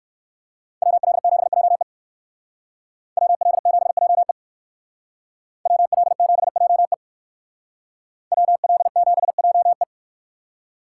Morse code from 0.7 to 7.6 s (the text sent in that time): WL7JE WL7JE WL7JE